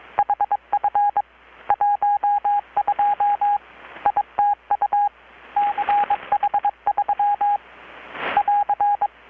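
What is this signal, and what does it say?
Morse code, heard as a keyed tone, about 22 wpm, 820 Hz, with radio noise. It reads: HF 12 ITU CH3 <AR>
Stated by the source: Morse from KPH's Night of Nights XX maritime radiotelegraphy event, received in Chiapas, Mexico about 3,800 km away on a Kenwood R-600 with a grounded 10 m long-wire antenna.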